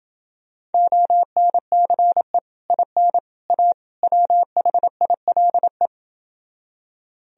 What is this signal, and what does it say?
Morse code, heard as a keyed tone, about 30 wpm, 700 Hz. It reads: ONCE IN A WHILE